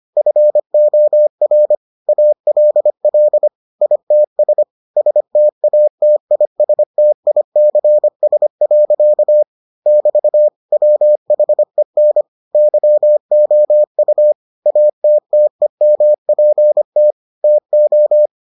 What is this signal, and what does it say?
Morse code, heard as a keyed tone, about 25 wpm, 600 Hz. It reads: FOR ALL ITS STATISTICS. <BT> WHEN YOU ATTEMPT TO